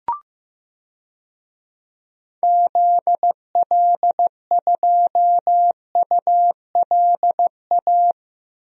Morse code, heard as a keyed tone, about 15 wpm, 700 Hz. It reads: ZL2ULA
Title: DXCC-100 Call Signs - 6 Characters - Repeated 2x - 15wpm